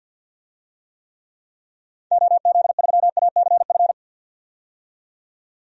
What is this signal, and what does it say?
Morse code, heard as a keyed tone, about 49 wpm, 700 Hz. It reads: OZ3ACF
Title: DXCC-100 Call Signs - 6 Characters - 50wpm